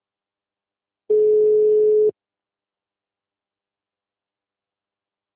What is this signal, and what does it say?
Ringback tone (Europe, 425 Hz).